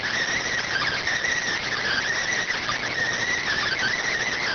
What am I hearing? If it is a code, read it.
Emergency Alert System (SAME) tones, heard as harsh buzzing bursts. It is heard from 0.6 to 4.1 s.